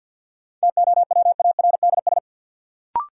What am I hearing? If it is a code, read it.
Morse code, heard as a keyed tone, about 53 wpm, 700 Hz. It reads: TOWARDS
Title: Sentences from Top 1000 Words - Review Part 2 - New Word Drill - 50wpm